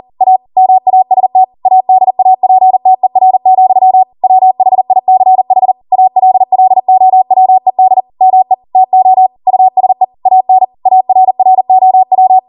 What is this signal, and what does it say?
Morse code, heard as a keyed tone, about 40 wpm, 750 Hz. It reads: A MAST ADAPTER, WHICH ALLOWED ME TO USE AN ARROW
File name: Morse / 40WPM